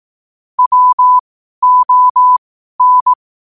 Morse code, heard as a keyed tone, about 18 wpm, spaced as for 12 wpm, 1000 Hz. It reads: WON
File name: qotdfiles / 12wpm